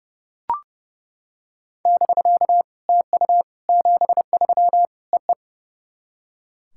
Morse code, heard as a keyed tone, about 30 wpm, 690 Hz. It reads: <BK> TU 73 EE
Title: SOTA QSO Practice 30wpm